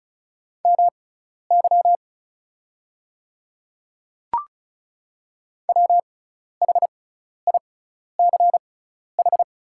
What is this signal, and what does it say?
Morse code, heard as a keyed tone, about 36 wpm, spaced as for 12 wpm, 700 Hz. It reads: MY WHICH
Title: 6x Character Spacing - Top 100 Words - 35wpm